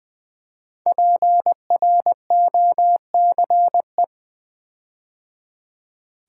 Morse code, heard as a keyed tone, about 20 wpm, 700 Hz. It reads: PROCE
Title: Build A Word - 5 Letters At A Time - 10-Letter Words - 20wpm